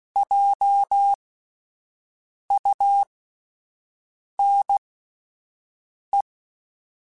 Morse code, heard as a keyed tone, about 16 wpm, spaced as for 5 wpm, 800 Hz. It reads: JUNE